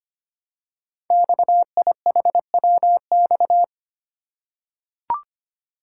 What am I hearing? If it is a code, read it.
Morse code, heard as a keyed tone, about 25 wpm, 700 Hz. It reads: XIHWX
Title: Code Groups - 5 Letters - 25wpm